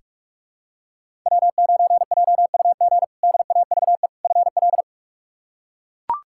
Morse code, heard as a keyed tone, about 44 wpm, 700 Hz. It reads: W9JUG DAVE FL
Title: K1USN SST QSO Practice - 45wpm